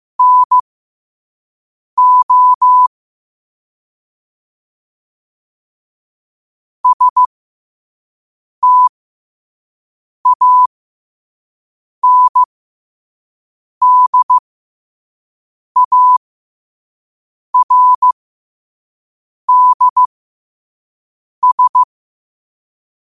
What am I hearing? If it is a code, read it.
Morse code, heard as a keyed tone, about 15 wpm, spaced as for 5 wpm, 1000 Hz. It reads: NO STANDARDS
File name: qotdfiles / 5wpm